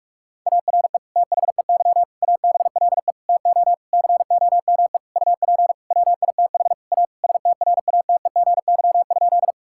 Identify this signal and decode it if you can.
Morse code, heard as a keyed tone, about 46 wpm, 700 Hz. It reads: ARE THEY ABLE TO COME UP WITH A STRATEGY?
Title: Sentences from Top 1000 Words - Review Part 5 - 45wpm